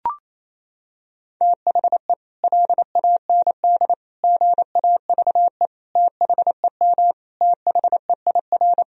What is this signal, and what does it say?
Morse code, heard as a keyed tone, about 28 wpm, 700 Hz. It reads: THE LAND GAVE THEM THEIR
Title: Sentences from Top 1000 Words - Review Part 1 - 28wpm